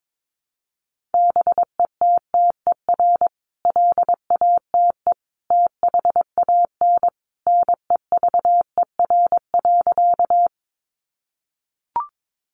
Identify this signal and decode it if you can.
Morse code, heard as a keyed tone, about 22 wpm, 700 Hz. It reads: BETTER LATE THAN NEVER.
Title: Sentences from Top 400 Words 22wpm